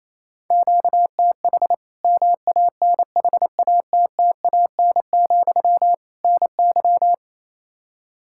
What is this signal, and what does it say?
Morse code, heard as a keyed tone, about 28 wpm, 700 Hz. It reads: QTH MANHATTAN, NY
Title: Practice QSO - Exchange 1 - Beginning - 28wpm